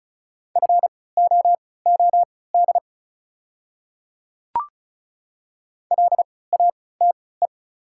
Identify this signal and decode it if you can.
Morse code, heard as a keyed tone, about 34 wpm, spaced as for 20 wpm, 690 Hz. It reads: FOOD LATE
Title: 3x Character Spacing - Top 500 Words - 35wpm